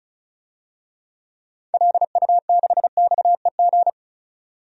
Morse code, heard as a keyed tone, about 34 wpm, 700 Hz. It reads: LU6XEG
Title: DXCC-100 Call Signs - 6 Characters - 35wpm